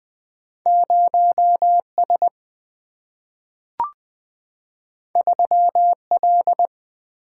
Morse code, heard as a keyed tone, about 20 wpm, 700 Hz. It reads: 0 S 3 L
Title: Single-Letter Single-Number 20wpm